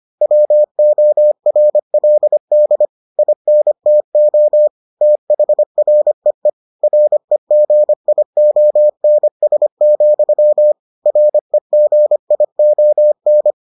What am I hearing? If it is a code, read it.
Morse code, heard as a keyed tone, about 25 wpm, 600 Hz. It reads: WORLD INTO THREE REGIONS, REGION